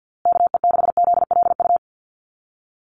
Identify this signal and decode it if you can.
Morse code, heard as a keyed tone, about 50 wpm, 700 Hz. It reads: KE6ZLU
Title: USA Call Signs Repeated 2x 50wpm